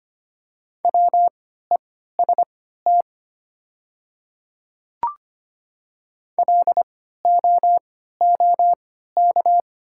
Morse code, heard as a keyed tone, about 25 wpm, spaced as for 14 wpm, 700 Hz. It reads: WEST LOOK
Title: 3x Character Spacing - Top 500 Words - 25wpm